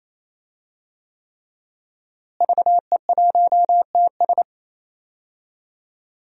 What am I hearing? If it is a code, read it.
Morse code, heard as a keyed tone, about 27 wpm, 700 Hz. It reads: VE1TS
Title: DXCC-100 Call Signs - 5 Characters - Repeated 3x - 28wpm